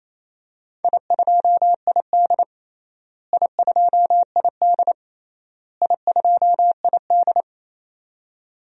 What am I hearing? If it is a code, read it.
Morse code, heard as a keyed tone, about 28 wpm, 700 Hz. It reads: I2ID I2ID I2ID
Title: DXCC-100 Call Signs - 4 Characters - Repeated 3x - 28wpm